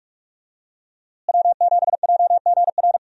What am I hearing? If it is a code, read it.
Morse code, heard as a keyed tone, about 46 wpm, 700 Hz. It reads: W7JGR
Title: USA Call Signs - 5 Characters - Repeated 3x - 45wpm